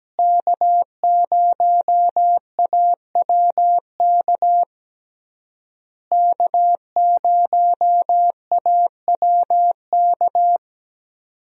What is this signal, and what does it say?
Morse code, heard as a keyed tone, about 17 wpm, 700 Hz. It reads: K0AWK K0AWK
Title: USA Call Signs - 5 Characters - Repeated 3x - 17wpm